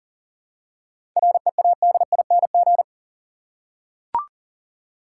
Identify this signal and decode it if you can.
Morse code, heard as a keyed tone, about 40 wpm, 700 Hz. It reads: READING
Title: Seven Letter Words - 40wpm